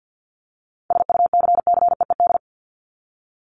Morse code, heard as a keyed tone, about 50 wpm, 700 Hz. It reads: SUCCEED
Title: Top 1501-1750 Words 50wpm